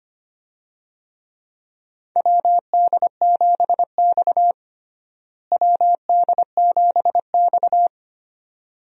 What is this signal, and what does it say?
Morse code, heard as a keyed tone, about 25 wpm, 700 Hz. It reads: WD7X WD7X
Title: USA Call Signs - 4 Characters - Repeated 2x - 25wpm